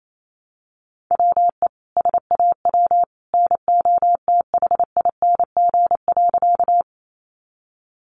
Morse code, heard as a keyed tone, about 28 wpm, 700 Hz. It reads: WE SAW NOTHING.
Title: A Sound Of Thunder 28wpm